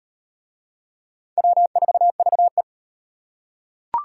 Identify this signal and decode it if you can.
Morse code, heard as a keyed tone, about 38 wpm, 700 Hz. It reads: W4VE